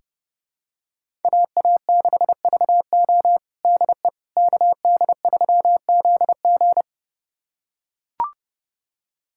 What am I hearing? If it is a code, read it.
Morse code, heard as a keyed tone, about 32 wpm, 700 Hz. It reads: AA6VO DE KD3ZG
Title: USA Callsign DE Callsign - 5 Characters - 30wpm